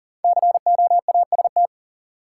Morse code, heard as a keyed tone, about 40 wpm, 700 Hz. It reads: COAST
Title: Sentences from Top 1000 Words - Review Part 4 - New Word Drill - 40wpm